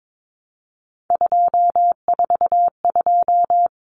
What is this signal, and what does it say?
Morse code, heard as a keyed tone, about 22 wpm, 700 Hz. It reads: 242
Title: Numbers - 3-Digit 22wpm